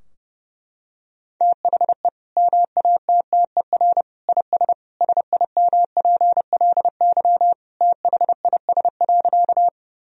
Morse code, heard as a keyed tone, about 31 wpm, 700 Hz. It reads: THE MATTER IS SIMPLY THIS.
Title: Sentences from Top 1000 Words - Review Part 4 - 30wpm